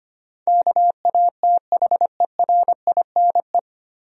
Morse code, heard as a keyed tone, about 25 wpm, 700 Hz. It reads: KATHERINE
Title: USA Common Names - 25wpm